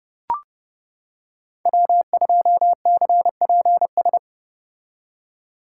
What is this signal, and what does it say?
Morse code, heard as a keyed tone, about 29 wpm, 700 Hz. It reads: W2CPS